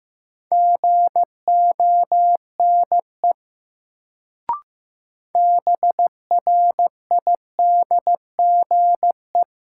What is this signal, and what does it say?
Morse code, heard as a keyed tone, about 15 wpm, 700 Hz. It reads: GONE BRIDGE
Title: Sentences from Top 1200 Words - New Word Drill - 15wpm